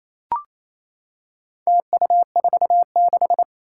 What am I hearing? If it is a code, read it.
Morse code, heard as a keyed tone, about 27 wpm, 700 Hz. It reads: TU46
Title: Four Challenging Letter-Number Combinations 28wpm